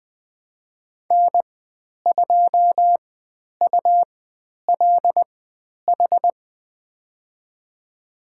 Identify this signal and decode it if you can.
Morse code, heard as a keyed tone, about 20 wpm, spaced as for 10 wpm, 700 Hz. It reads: N2ULH
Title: 109 Lesson 18g - Letter m - Call Signs 20wpm